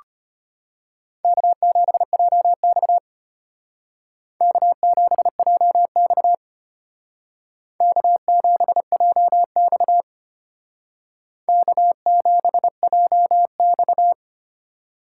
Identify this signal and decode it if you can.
Morse code, heard as a keyed tone, about 33 wpm, 700 Hz. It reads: K7JX K7JX K7JX K7JX